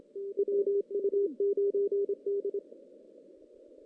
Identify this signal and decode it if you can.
Morse code, heard as a keyed tone, about 28 wpm, 400 Hz. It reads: WU9D
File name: 2016 JAN CW NAQP / 2016 Jan naqp k5ot - 0-5